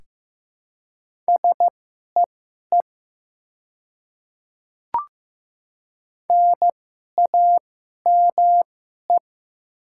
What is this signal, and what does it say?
Morse code, heard as a keyed tone, about 15 wpm, spaced as for 11 wpm, 700 Hz. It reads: SEE NAME
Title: 2x Character Spacing - Top 200 Words - 15wpm